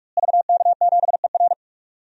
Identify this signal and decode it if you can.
Morse code, heard as a keyed tone, about 46 wpm, 700 Hz. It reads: VK7ER
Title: DXCC-100 Call Signs - 5 Characters - Repeated 2x - 45wpm